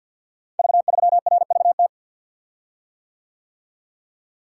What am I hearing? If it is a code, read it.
Morse code, heard as a keyed tone, about 51 wpm, 700 Hz. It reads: V3RVT
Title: Code Groups - 5 Letter:Numbers - 50wpm